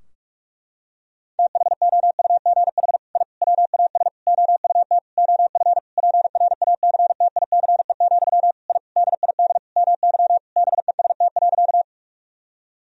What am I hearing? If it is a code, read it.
Morse code, heard as a keyed tone, about 46 wpm, 700 Hz. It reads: THOUGH I WAS OUT OF PRACTICE, I DID MY BEST.
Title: Sentences from Top 1000 Words - Review Part 4 - 45wpm